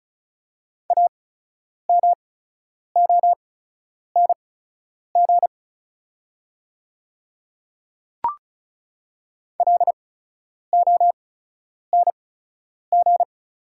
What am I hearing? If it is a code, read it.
Morse code, heard as a keyed tone, about 36 wpm, spaced as for 10 wpm, 700 Hz. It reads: AMONG LONG
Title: 8x Character Spacing - Top 500 Words - 35wpm